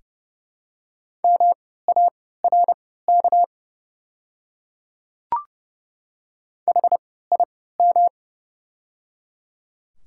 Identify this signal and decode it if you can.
Morse code, heard as a keyed tone, about 30 wpm, spaced as for 17 wpm, 690 Hz. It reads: MARK HIM
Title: 3x Character Spacing - Top 300 Words - 30wpm